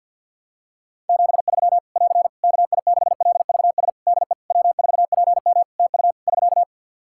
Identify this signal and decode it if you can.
Morse code, heard as a keyed tone, about 48 wpm, 700 Hz. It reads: 73 <AR> KI6LVS DE W4PM TU <SK>
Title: Practice QSO - Exchange 4 - Ending - 50wpm